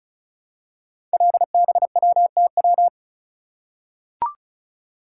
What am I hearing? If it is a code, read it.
Morse code, heard as a keyed tone, about 36 wpm, 700 Hz. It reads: LBWTW
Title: Code Groups - 5 Letter:Numbers - 35wpm